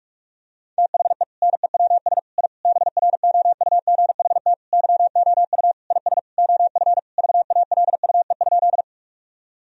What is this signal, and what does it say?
Morse code, heard as a keyed tone, about 46 wpm, 700 Hz. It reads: THE NEWS I BROUGHT YOU IS OF VALUE?
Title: Sentences from Top 1000 Words - Review Part 3 - 45wpm